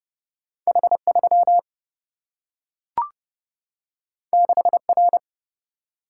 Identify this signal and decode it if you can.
Morse code, heard as a keyed tone, about 30 wpm, 700 Hz. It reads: H3 6R
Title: Single-Letter Single-Number 30wpm